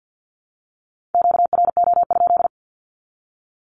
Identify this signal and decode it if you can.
Morse code, heard as a keyed tone, about 50 wpm, 700 Hz. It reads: QRO?
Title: Q-Codes with Spoken Meaning - 50wpm